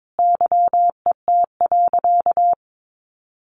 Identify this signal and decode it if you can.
Morse code, heard as a keyed tone, about 22 wpm, 700 Hz. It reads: YET.
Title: Sentences from Top 1100 Words 22wpm